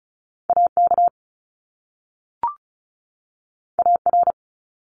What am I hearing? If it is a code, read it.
Morse code, heard as a keyed tone, about 36 wpm, 700 Hz. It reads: AK AR
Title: US States 35wpm